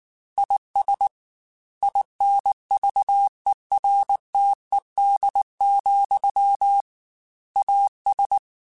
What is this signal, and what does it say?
Morse code, heard as a keyed tone, about 19 wpm, 800 Hz. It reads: IS INVERTED, AS
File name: Technical Subjects / Technical Subjects-mp3s / XC @18